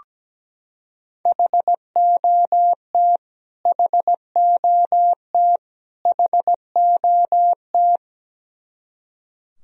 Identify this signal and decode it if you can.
Morse code, heard as a keyed tone, about 17 wpm, 700 Hz. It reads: HOT HOT HOT
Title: Top 100 Words Repeated 17wpm